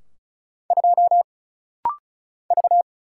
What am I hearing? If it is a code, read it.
Morse code, heard as a keyed tone, about 34 wpm, 700 Hz. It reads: 2 V
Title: Single Letter-Number - Rapid-Fire 35wpm